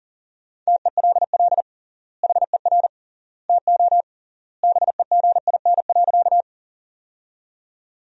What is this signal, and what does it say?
Morse code, heard as a keyed tone, about 40 wpm, 700 Hz. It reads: TELL HER TO BEGIN.
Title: 3x Word-Spacing - Sentences from Top 300 Words - 40wpm